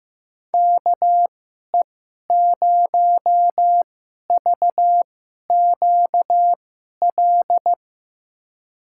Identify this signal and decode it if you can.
Morse code, heard as a keyed tone, about 15 wpm, spaced as for 11 wpm, 700 Hz. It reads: KE0VQL
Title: USA Call Signs - 6 Characters - 2x Character Spacing - 15wpm